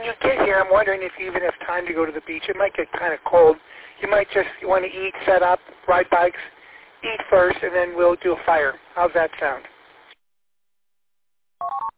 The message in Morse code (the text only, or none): none